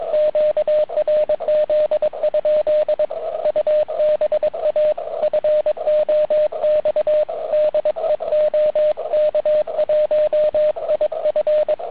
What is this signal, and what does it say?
Morse code, heard as a keyed tone, about 22 wpm, 620 Hz, with radio noise. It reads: QRZ? UBA FOX DE OK1IF